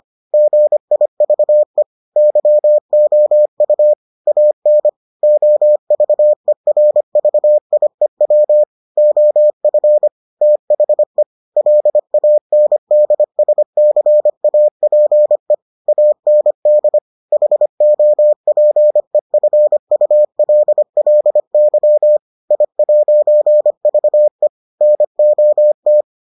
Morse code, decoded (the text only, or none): GIVE YOU AN OVERVIEW OF THE LANDSCAPE AND HOPEFULLY I'VE NOT